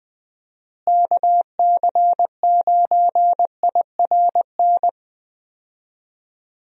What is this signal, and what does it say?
Morse code, heard as a keyed tone, about 20 wpm, 700 Hz. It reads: KC9IRN
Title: USA Call Signs - 6 Characters - Repeated 3x - 20wpm